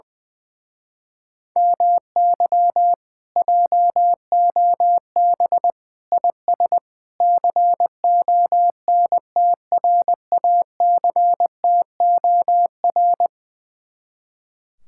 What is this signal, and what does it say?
Morse code, heard as a keyed tone, about 20 wpm, 700 Hz. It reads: MY JOB IS CONTRACTOR
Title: Speed-Racing Common QSO Phrases - 20wpm